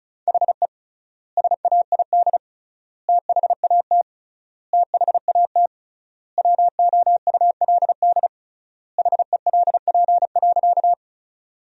Morse code, heard as a keyed tone, about 35 wpm, 700 Hz. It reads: HE SAID THAT THAT WOULD HELP.
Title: 3x Word-Spacing - Sentences from Top 200 Words - 35wpm